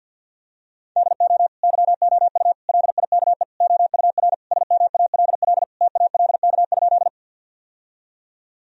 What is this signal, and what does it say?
Morse code, heard as a keyed tone, about 51 wpm, 700 Hz. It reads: DO YOU LIKE OUR SMALL TALK?
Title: Sentences from Top 1000 Words - Review Part 1 - 50wpm